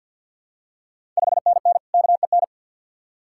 Morse code, heard as a keyed tone, about 47 wpm, 690 Hz. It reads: 5NN KEN
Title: Ontario QSO Party Exchange 50wpm